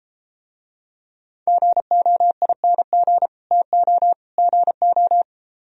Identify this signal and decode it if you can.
Morse code, heard as a keyed tone, about 32 wpm, 700 Hz. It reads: GOING TO GO